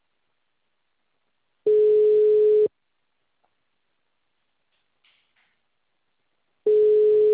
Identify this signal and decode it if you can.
Ringback tone (Europe, 425 Hz).
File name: out- -2003- - - .89477